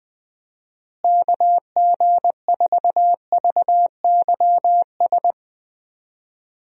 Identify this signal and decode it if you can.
Morse code, heard as a keyed tone, about 19 wpm, 700 Hz. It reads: KG4VYS
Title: USA Call Signs - 6 Characters - Repeated 2x - 20wpm